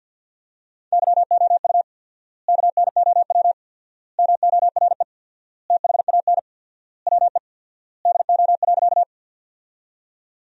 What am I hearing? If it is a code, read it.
Morse code, heard as a keyed tone, about 53 wpm, 700 Hz. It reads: YOU KNOW MORE THAN WE DO.
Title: 4x Word-Spacing - Sentences from Top 100 Words - 50wpm